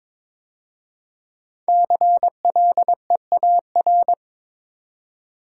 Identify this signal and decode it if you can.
Morse code, heard as a keyed tone, about 22 wpm, 700 Hz. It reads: CLEAR
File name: 5 Letter Words - First 3 Letters Given - 22wpm